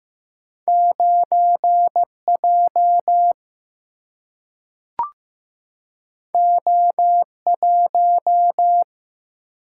Morse code, heard as a keyed tone, about 15 wpm, 700 Hz. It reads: 9 J O 1
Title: Single-Letter Single-Number 15wpm